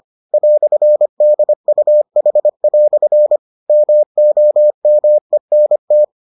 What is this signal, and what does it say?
Morse code, heard as a keyed tone, about 25 wpm, 600 Hz. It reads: "DUH" MOMENT